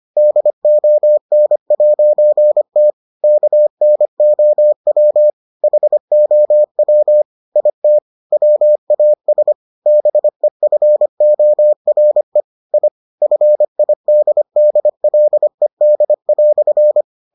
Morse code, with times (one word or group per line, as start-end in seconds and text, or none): 0.2-2.9 DON'T
3.2-5.3 KNOW
5.6-7.2 HOW
7.6-8.0 IT
8.3-9.5 WAS
9.9-12.4 BEFORE
12.7-12.9 I
13.2-17.0 FIDDLED"